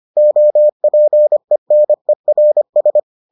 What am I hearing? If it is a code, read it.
Morse code, heard as a keyed tone, about 25 wpm, 600 Hz. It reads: OPENERS